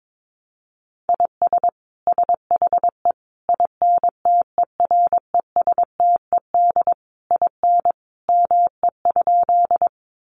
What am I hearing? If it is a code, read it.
Morse code, heard as a keyed tone, about 22 wpm, 700 Hz. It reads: IS SHE INTERESTED IN ME?
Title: Sentences from Top 900 Words 22wpm